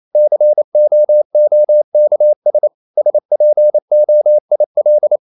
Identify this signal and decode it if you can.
Morse code, heard as a keyed tone, about 28 wpm, 600 Hz. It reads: COOKS SPOIL
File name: CW / Practice / EnglishSayings28